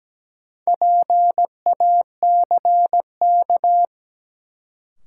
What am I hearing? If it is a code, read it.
Morse code, heard as a keyed tone, about 17 wpm, 700 Hz. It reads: PACK
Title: Sentences from Top 1300 Words - New Word Drill - 17wpm